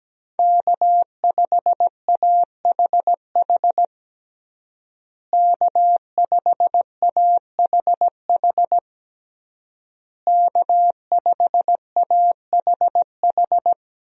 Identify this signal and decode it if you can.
Morse code, heard as a keyed tone, about 17 wpm, 700 Hz. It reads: K5AHH K5AHH K5AHH
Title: USA Call Signs - 5 Characters - Repeated 3x - 17wpm